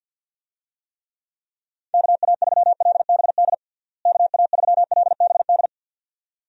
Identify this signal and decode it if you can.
Morse code, heard as a keyed tone, about 50 wpm, 700 Hz. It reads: KA3LBD KA3LBD
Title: USA Call Signs - 6 Characters - Repeated 2x - 50wpm